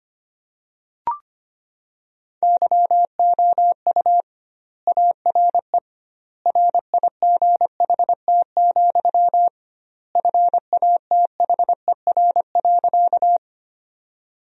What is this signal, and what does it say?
Morse code, heard as a keyed tone, about 25 wpm, 700 Hz. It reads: YOU ARE RIGHT, FATHER.
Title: 2x Word-Spacing - Sentences from Top 300 Words - 25wpm